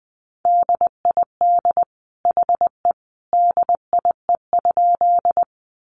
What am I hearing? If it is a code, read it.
Morse code, heard as a keyed tone, about 20 wpm, 700 Hz. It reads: DID HE DIE?
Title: Sentences from Top 1300 Words 20wpm